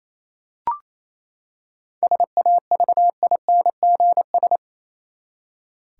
Morse code, heard as a keyed tone, about 28 wpm, 700 Hz. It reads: SAVINGS
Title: Seven Letter Words - 28wpm